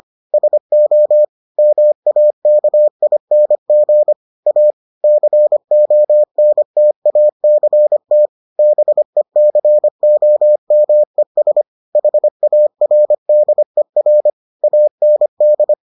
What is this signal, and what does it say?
Morse code, heard as a keyed tone, about 25 wpm, 600 Hz. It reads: SO MAKING A CONTACT BECOMES HARDER AND